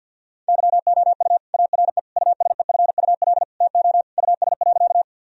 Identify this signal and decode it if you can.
Morse code, heard as a keyed tone, about 48 wpm, 700 Hz. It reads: YOU ARE USEFUL TO US.
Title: Sentences from Top 1000 Words - Review Part 4 - 50wpm